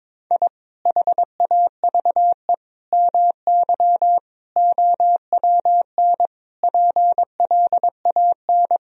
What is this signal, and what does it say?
Morse code, heard as a keyed tone, about 22 wpm, 700 Hz. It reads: I HAVE MY OWN PLAN
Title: Sentences from Top 1000 Words - Review Part 2 - 22wpm